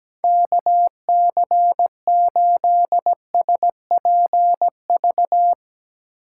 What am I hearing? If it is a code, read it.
Morse code, heard as a keyed tone, about 17 wpm, 700 Hz. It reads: KC8SPV
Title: USA Call Signs - 6 Characters - 3x Character Spacing - 17wpm